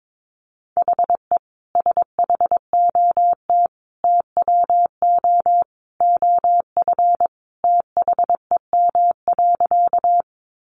Morse code, heard as a keyed tone, about 22 wpm, 700 Hz. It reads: HE SHOT TWO OF THEM.
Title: Sentences from Top 1100 Words 22wpm